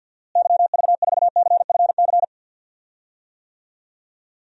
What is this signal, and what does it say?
Morse code, heard as a keyed tone, about 51 wpm, 700 Hz. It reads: YV4CFC
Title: DXCC-100 Call Signs - 6 Characters - Repeated 2x - 50wpm